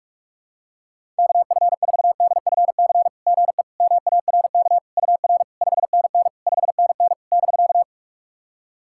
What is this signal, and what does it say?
Morse code, heard as a keyed tone, about 44 wpm, 690 Hz. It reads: KF4DFC GE MARK UR 5NN 5NN <BK>
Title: SOTA QSO Practice 45wpm